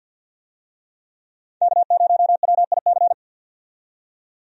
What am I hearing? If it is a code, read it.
Morse code, heard as a keyed tone, about 52 wpm, 700 Hz. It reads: K0WIC
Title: USA Call Signs - 5 Characters - 50wpm